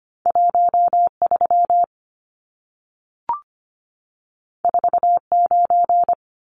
Morse code, heard as a keyed tone, about 25 wpm, 700 Hz. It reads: 13 49
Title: Numbers - 2-Digit 25wpm